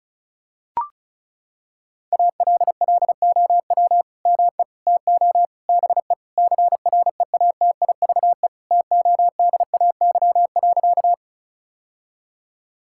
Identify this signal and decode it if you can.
Morse code, heard as a keyed tone, about 36 wpm, 700 Hz. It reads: ALLOW ME TO BE CREATIVE TODAY.
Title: Sentences from Top 1000 Words - Review Part 5 - 35wpm